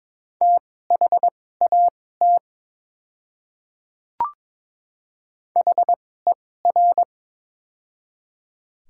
Morse code, heard as a keyed tone, about 22 wpm, spaced as for 16 wpm, 700 Hz. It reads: THAT HER
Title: 2x Character Spacing - Top 100 Words - 22wpm